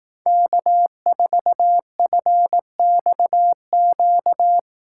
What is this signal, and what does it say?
Morse code, heard as a keyed tone, about 18 wpm, 700 Hz. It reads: K4FXQ
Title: Speed Racing - USA 1x3 Call Signs 15wpm